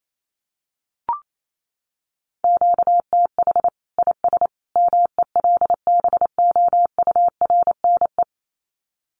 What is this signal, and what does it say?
Morse code, heard as a keyed tone, about 28 wpm, 700 Hz. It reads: QTH IS MELBOURNE
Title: Common QSO Phrases 28wpm